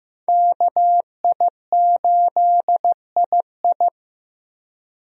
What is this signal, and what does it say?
Morse code, heard as a keyed tone, about 15 wpm, 700 Hz. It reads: KI8II
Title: USA Call Signs - 5 Characters - 15wpm